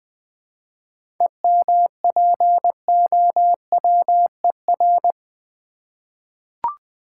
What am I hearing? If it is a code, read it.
Morse code, heard as a keyed tone, about 20 wpm, 700 Hz. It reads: EMPOWER
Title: 7 Letter Words - First 3 Letters Given - 20wpm